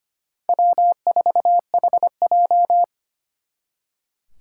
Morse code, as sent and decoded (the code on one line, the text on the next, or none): .-- ....- .... .---
W4HJ